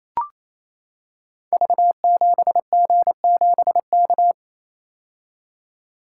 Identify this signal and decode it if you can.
Morse code, heard as a keyed tone, about 29 wpm, 700 Hz. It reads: V7G7K
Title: Code Groups - 5 Letter:Numbers - 28wpm